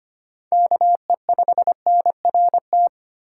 Morse code, heard as a keyed tone, about 25 wpm, 700 Hz. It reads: KE5NRT
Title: USA Call Signs - 3x Character Spacing - 25wpm